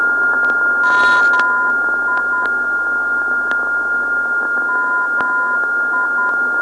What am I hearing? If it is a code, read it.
Morse code, heard as a keyed tone, about 10 wpm, 1010 Hz, with noise. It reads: MI MI
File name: ndb zvok / MI-355khz